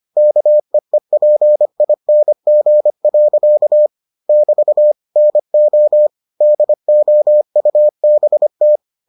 Morse code, heard as a keyed tone, about 25 wpm, 600 Hz. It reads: KEEPING. <BT> NO DOUBT